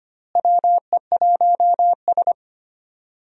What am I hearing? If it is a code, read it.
Morse code, heard as a keyed tone, about 25 wpm, 700 Hz. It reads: WE1S
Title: USA Call Signs - 4 Characters - Repeated 2x - 25wpm